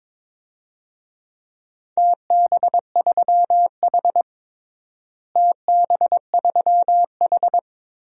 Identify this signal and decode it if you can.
Morse code, heard as a keyed tone, about 22 wpm, 700 Hz. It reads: TB3H TB3H
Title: DXCC-100 Call Signs - 4 Characters - Repeated 3x - 22wpm